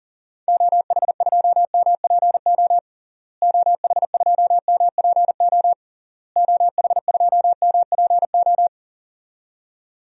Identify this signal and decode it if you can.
Morse code, heard as a keyed tone, about 37 wpm, 700 Hz. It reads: OH2MPO OH2MPO OH2MPO
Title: DXCC-100 Call Signs - 6 Characters - Repeated 3x - 40wpm